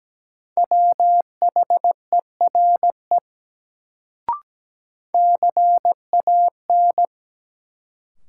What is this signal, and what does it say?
Morse code, heard as a keyed tone, about 17 wpm, 700 Hz. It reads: WHERE CAN
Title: Sets of 2 Words - New Word Drill - 17wpm